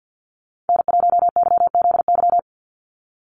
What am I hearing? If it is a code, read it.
Morse code, heard as a keyed tone, about 49 wpm, 700 Hz. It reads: N1YZY
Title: USA Call Signs 50wpm